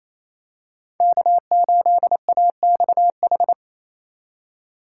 Morse code, heard as a keyed tone, about 28 wpm, 700 Hz. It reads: K8AXH
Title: USA Call Signs - 5 Characters - 4x Character Spacing - 28wpm